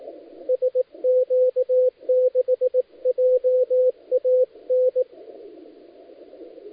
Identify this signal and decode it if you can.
Morse code, heard as a keyed tone, about 19 wpm, 510 Hz, with radio noise. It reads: SQ6JAN
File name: dlazaawansowanych / nagraniazpasma / titawaNETcontrol